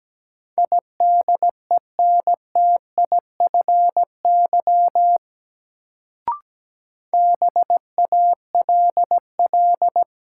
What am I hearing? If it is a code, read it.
Morse code, heard as a keyed tone, about 17 wpm, 700 Hz. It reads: IDENTIFY BALL
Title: Sentences from Top 900 Words - New Word Drill - 17wpm